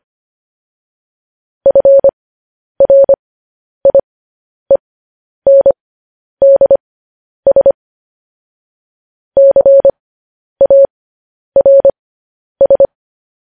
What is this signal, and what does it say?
Morse code, heard as a keyed tone, about 25 wpm, spaced as for 10 wpm, 550 Hz. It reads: FRIENDS CARS
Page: news-headlines-in-morse-code-at-10-wpm